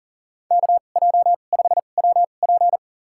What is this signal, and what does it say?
Morse code, heard as a keyed tone, about 39 wpm, spaced as for 29 wpm, 700 Hz. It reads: KJ5WP
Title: USA Call Signs - 5 Characters - 2x Character Spacing - 40wpm